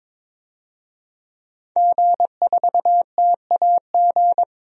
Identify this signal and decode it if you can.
Morse code, heard as a keyed tone, about 22 wpm, 700 Hz. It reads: G4TAG
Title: DXCC-100 Call Signs - 5 Characters - 22wpm